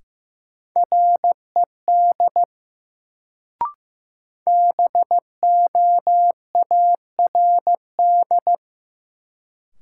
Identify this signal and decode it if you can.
Morse code, heard as a keyed tone, about 15 wpm, 700 Hz. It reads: RED BOARD
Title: Sentences from Top 700 Words - New Word Drill - 15wpm